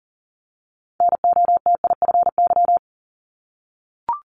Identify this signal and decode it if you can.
Morse code, heard as a keyed tone, about 41 wpm, 700 Hz. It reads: NOTIFY
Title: Multi-letter Phonemes 40wpm